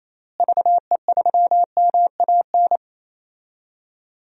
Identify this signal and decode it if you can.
Morse code, heard as a keyed tone, about 28 wpm, 700 Hz. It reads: VE3MAN
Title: DXCC-100 Call Signs - 6 Characters - Repeated 2x - 28wpm